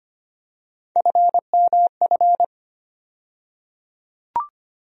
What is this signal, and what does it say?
Morse code, heard as a keyed tone, about 25 wpm, 700 Hz. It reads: FMF
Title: USA Call Sign Postfixes - 3 Letters 25wpm